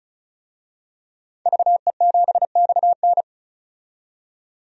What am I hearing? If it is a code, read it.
Morse code, heard as a keyed tone, about 34 wpm, 700 Hz. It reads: VE7XN
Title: DXCC-100 Call Signs - 5 Characters - 35wpm